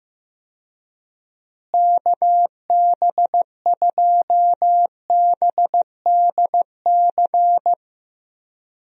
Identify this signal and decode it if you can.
Morse code, heard as a keyed tone, about 15 wpm, 700 Hz. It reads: KB2BDC
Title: USA Call Signs - 6 Characters - Repeated 2x - 15wpm